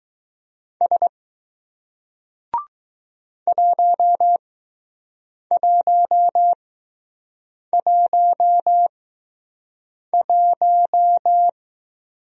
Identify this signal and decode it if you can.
Morse code, heard as a keyed tone, about 20 wpm, spaced as for 6 wpm, 700 Hz. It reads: S 1111